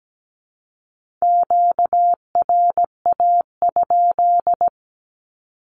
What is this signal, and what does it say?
Morse code, heard as a keyed tone, about 17 wpm, 700 Hz. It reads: QRA?
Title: Q-Codes with Spoken Meaning - 17wpm